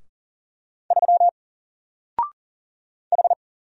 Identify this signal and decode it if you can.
Morse code, heard as a keyed tone, about 40 wpm, 700 Hz. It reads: 3 H
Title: Single Character (Letter:Number:Punctuation:Prosign) Alt 40wpm